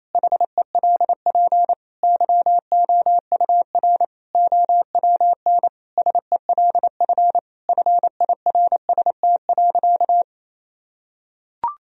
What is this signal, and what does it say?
Morse code, heard as a keyed tone, about 28 wpm, 700 Hz. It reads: HELP YOUR OWN SELF FIRST.
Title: Sentences from Top 1000 Words - Review Part 1 - 28wpm